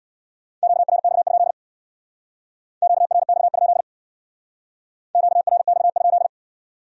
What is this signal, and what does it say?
Morse code, heard as a keyed tone, about 62 wpm, 700 Hz. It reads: QRX? QRX? QRX?
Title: Speed-Racing Q-Codes - 50wpm